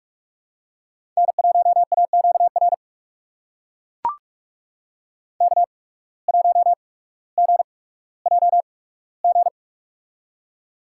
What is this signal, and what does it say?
Morse code, heard as a keyed tone, about 41 wpm, spaced as for 19 wpm, 700 Hz. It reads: N1AQR K1GJG
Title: USA Call Signs - 5 Characters - 8x Character Spacing - 45wpm